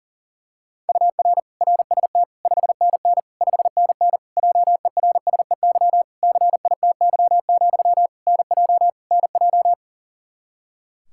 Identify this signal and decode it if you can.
Morse code, heard as a keyed tone, about 40 wpm, 700 Hz. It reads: UR RST 5NN 5NN JERSEY CITY, NJ NJ
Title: Practice QSO - Exchange 1 - Beginning - 40wpm